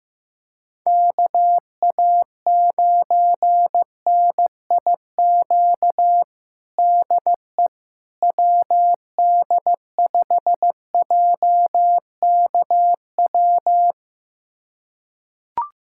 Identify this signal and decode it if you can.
Morse code, heard as a keyed tone, about 15 wpm, 700 Hz. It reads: KA9NIQ DE WD5JKW
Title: USA Callsign DE Callsign - 6 Characters - 15wpm